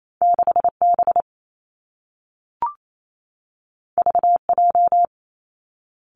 Morse code, heard as a keyed tone, about 28 wpm, 700 Hz. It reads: 6 B V J
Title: Two Challenging Letter-Number Combinations 28wpm